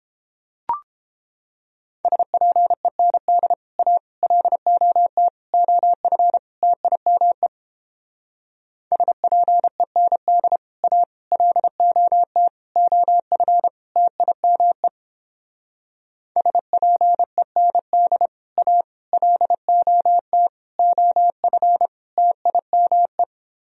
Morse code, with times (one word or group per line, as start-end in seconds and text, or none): 2.1-3.5 SPEND
3.8-4.0 A
4.2-5.3 LOT
5.5-6.4 OF
6.6-7.5 TIME
8.9-10.6 SPEND
10.8-11.0 A
11.3-12.5 LOT
12.8-13.7 OF
14.0-14.9 TIME
16.4-18.3 SPEND
18.6-18.8 A
19.1-20.5 LOT
20.8-21.9 OF
22.2-23.2 TIME